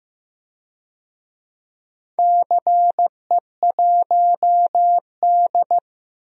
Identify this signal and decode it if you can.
Morse code, heard as a keyed tone, about 15 wpm, 700 Hz. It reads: CE1D
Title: DXCC-100 Call Signs - 4 Characters - Repeated 3x - 15wpm